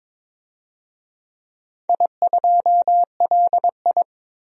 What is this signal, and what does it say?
Morse code, heard as a keyed tone, about 22 wpm, 700 Hz. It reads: I2LI
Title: DXCC-100 Call Signs - 4 Characters - 22wpm